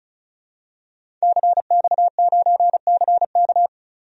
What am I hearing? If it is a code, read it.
Morse code, heard as a keyed tone, about 36 wpm, 700 Hz. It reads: CX9CK
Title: DXCC-100 Call Signs - 5 Characters - Repeated 2x - 35wpm